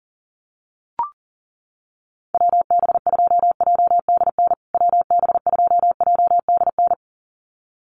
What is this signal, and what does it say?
Morse code, heard as a keyed tone, about 40 wpm, 700 Hz. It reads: WB2JDN WB2JDN
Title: USA Call Signs Repeated 2x 40wpm